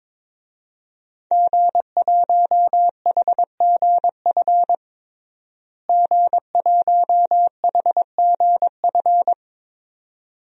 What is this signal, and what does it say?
Morse code, heard as a keyed tone, about 22 wpm, 700 Hz. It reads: G1HGF G1HGF
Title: DXCC-100 Call Signs - 5 Characters - Repeated 2x - 22wpm